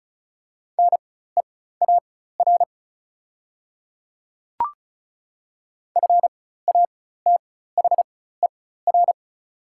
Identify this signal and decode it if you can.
Morse code, heard as a keyed tone, about 36 wpm, spaced as for 16 wpm, 700 Hz. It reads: NEAR FATHER
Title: 4x Character Spacing - Top 500 Words - 35wpm